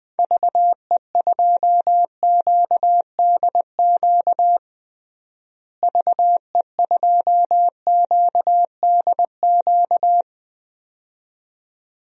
Morse code, heard as a keyed tone, about 21 wpm, 700 Hz. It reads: VE2QDQ VE2QDQ
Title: DXCC-100 Call Signs - 6 Characters - Repeated 2x - 20wpm